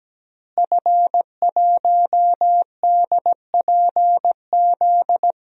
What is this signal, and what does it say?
Morse code, heard as a keyed tone, about 17 wpm, 700 Hz. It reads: F1DPZ
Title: DXCC-100 Call Signs - 5 Characters - Repeated 3x - 17wpm